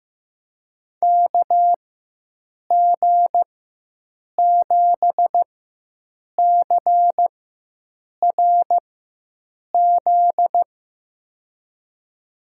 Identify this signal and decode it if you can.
Morse code, heard as a keyed tone, about 15 wpm, spaced as for 7 wpm, 700 Hz. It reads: KG7CRZ
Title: USA Call Signs - 6 Characters - 4x Character Spacing - 15wpm